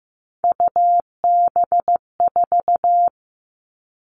Morse code, heard as a keyed tone, about 15 wpm, 700 Hz. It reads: UB4
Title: Three Challenging Letter-Number Combinations 15wpm